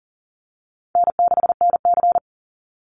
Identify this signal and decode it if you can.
Morse code, heard as a keyed tone, about 40 wpm, 700 Hz. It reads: N6NC
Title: Easy Call Signs - 40wpm